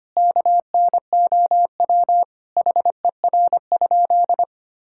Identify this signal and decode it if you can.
Morse code, heard as a keyed tone, about 25 wpm, 700 Hz. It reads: KNOW HER?
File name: practice / 25wpm / Sentences with 3-Word Phrases - 25wpm